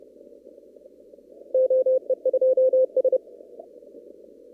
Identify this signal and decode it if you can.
Morse code, heard as a keyed tone, about 32 wpm, 520 Hz. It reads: OE2S
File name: CW / kt5j wpx cw-op-k5pi